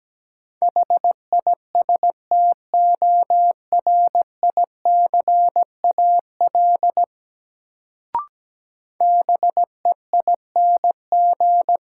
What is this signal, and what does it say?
Morse code, heard as a keyed tone, about 17 wpm, 700 Hz. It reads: HISTORICAL BEING
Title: Sentences from Top 600 Words - New Word Drill - 17wpm